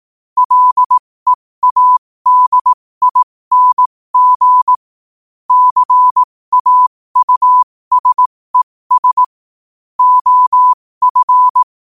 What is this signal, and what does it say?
Morse code, heard as a keyed tone, about 18 wpm, spaced as for 15 wpm, 1000 Hz. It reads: LEADING CAUSES OF
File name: qotdfiles / 15wpm